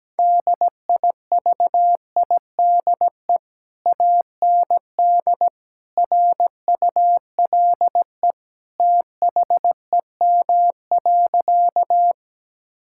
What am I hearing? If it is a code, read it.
Morse code, heard as a keyed tone, about 17 wpm, 700 Hz. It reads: DIVIDE AND RULE THEM.
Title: Sentences from Top 1600 Words 17wpm